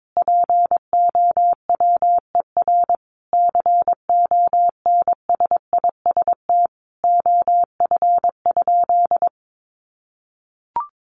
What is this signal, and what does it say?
Morse code, heard as a keyed tone, about 22 wpm, 700 Hz. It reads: POWER CONSIST OF?
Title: Sentences from Top 1400 Words 22wpm